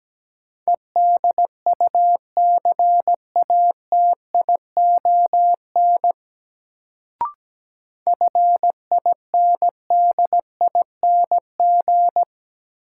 Morse code, heard as a keyed tone, about 17 wpm, 700 Hz. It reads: EDUCATION FINDING
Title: Sentences from Top 1100 Words - New Word Drill Alt - 17wpm